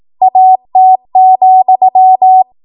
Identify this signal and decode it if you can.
Morse code, heard as a keyed tone, about 18 wpm, 750 Hz. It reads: AT,
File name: Morse / 18WPM